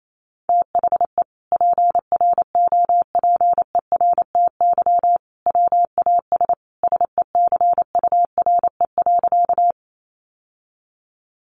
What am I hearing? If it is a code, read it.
Morse code, heard as a keyed tone, about 28 wpm, 700 Hz. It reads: THE PROPERTY WAS SECURE.
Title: Sentences from Top 1200 Words 28wpm